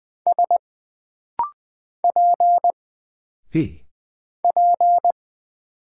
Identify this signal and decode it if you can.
Morse code, heard as a keyed tone, about 20 wpm, 700 Hz. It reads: S P P
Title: Single Character (Letter:Number:Punctuation:Prosign) - Rapid-Fire - 20wpm